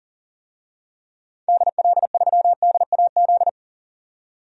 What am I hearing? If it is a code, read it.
Morse code, heard as a keyed tone, about 40 wpm, 700 Hz. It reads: DL3DAZ
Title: DXCC-100 Call Signs - 6 Characters - 40wpm